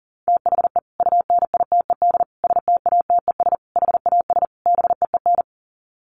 Morse code, heard as a keyed tone, about 40 wpm, 700 Hz. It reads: THE UNITED STATES HAS BEEN
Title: Sets of 5 Words Encore2 40wpm